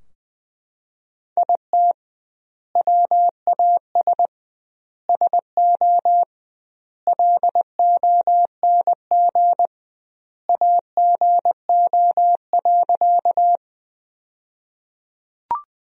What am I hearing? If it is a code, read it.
Morse code, heard as a keyed tone, about 21 wpm, 700 Hz. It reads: IT WAS SO LONG AGO.
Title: 2x Word-Spacing - Sentences from Top 500 Words - 20wpm